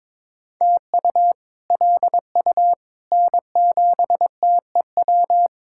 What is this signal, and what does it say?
Morse code, heard as a keyed tone, about 21 wpm, 700 Hz. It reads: TU LU N7TEW
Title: K1USN SST QSO Practice - 22wpm